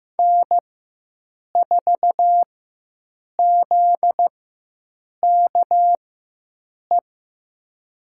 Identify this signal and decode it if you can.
Morse code, heard as a keyed tone, about 15 wpm, spaced as for 7 wpm, 700 Hz. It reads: N4ZKE
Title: USA Call Signs - 5 Characters - 4x Character Spacing - 15wpm